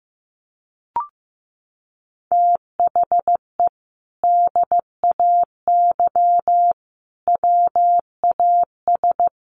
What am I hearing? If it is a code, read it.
Morse code, heard as a keyed tone, about 15 wpm, 700 Hz. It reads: THE DAY WAS
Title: Sentences from Top 1200 Words 15wpm